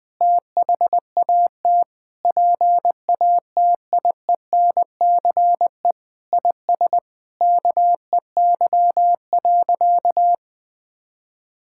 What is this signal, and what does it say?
Morse code, heard as a keyed tone, about 20 wpm, 700 Hz. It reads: THAT PATIENCE IS KEY.
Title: Sentences with 7-Word Phrases - 20wpm